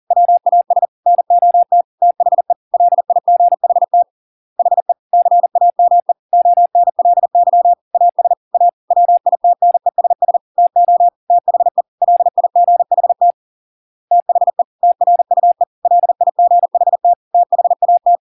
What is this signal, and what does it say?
Morse code, heard as a keyed tone, about 40 wpm, 700 Hz. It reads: WAS NOT THE LIGHT HE CAME ONLY AS A WITNESS TO THE LIGHT THE TRUE LIGHT THAT